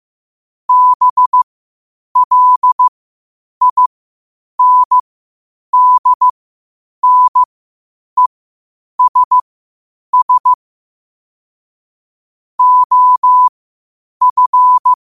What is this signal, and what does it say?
Morse code, heard as a keyed tone, about 15 wpm, spaced as for 8 wpm, 1000 Hz. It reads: BLINDNESS OF
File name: qotdfiles / 8wpm